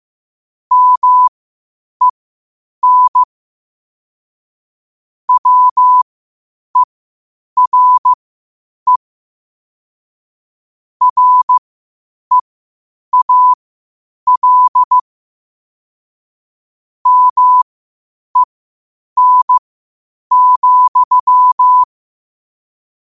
Morse code, heard as a keyed tone, about 15 wpm, spaced as for 8 wpm, 1000 Hz. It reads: MEN WERE REAL MEN,
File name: qotdfiles / 8wpm